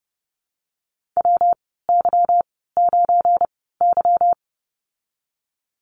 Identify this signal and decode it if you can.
Morse code, heard as a keyed tone, about 30 wpm, spaced as for 17 wpm, 700 Hz. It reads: WY9Y
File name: Lesson 37g - Letter x - Call Signs 30wpm